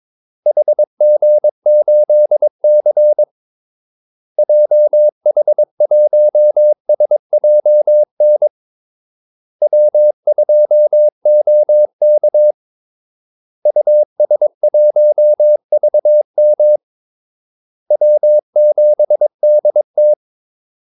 Morse code, heard as a keyed tone, about 22 wpm, 600 Hz. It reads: HG8C JH1SJN W2OK US1VM W7DT
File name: Callsigns-22wpm